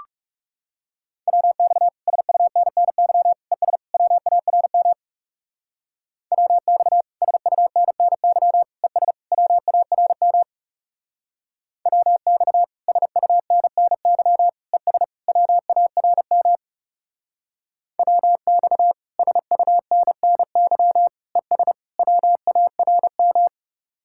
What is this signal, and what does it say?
Morse code, heard as a keyed tone, about 38 wpm, 700 Hz. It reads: WX SUNNY ES WARM WX SUNNY ES WARM WX SUNNY ES WARM WX SUNNY ES WARM